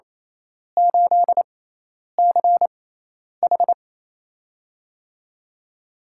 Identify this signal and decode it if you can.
Morse code, heard as a keyed tone, about 28 wpm, spaced as for 10 wpm, 700 Hz. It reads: 8CH
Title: USA Call Signs - 4 Characters - 6x Character Spacing - 28wpm